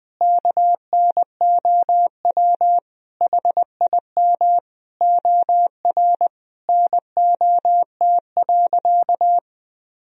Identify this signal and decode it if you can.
Morse code, heard as a keyed tone, about 20 wpm, 700 Hz. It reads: KNOW HIM OR NOT.
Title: Sentences from Top 1000 Words - Review Part 3 - 20wpm